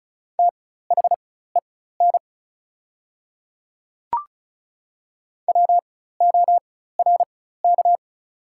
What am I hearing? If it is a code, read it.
Morse code, heard as a keyed tone, about 36 wpm, spaced as for 16 wpm, 690 Hz. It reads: THEN WORK
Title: 4x Character Spacing - Top 200 Words - 35wpm